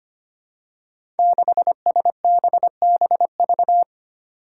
Morse code, heard as a keyed tone, about 25 wpm, 700 Hz. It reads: 6SBBV
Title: Five Challenging Letter-Number Combinations 25wpm